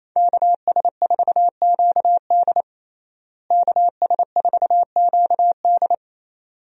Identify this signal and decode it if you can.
Morse code, heard as a keyed tone, about 28 wpm, 700 Hz. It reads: KS4QD KS4QD
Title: USA Call Signs - 5 Characters - Repeated 2x - 28wpm